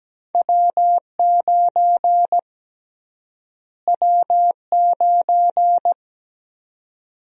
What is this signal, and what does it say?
Morse code, heard as a keyed tone, about 17 wpm, 700 Hz. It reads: W9 W9
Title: USA Call Sign Prefixes - Repeated 2x - 17wpm